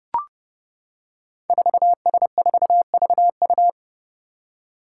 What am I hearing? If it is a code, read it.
Morse code, heard as a keyed tone, about 28 wpm, 700 Hz. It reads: 4S4VU
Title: Five Challenging Letter-Number Combinations 30wpm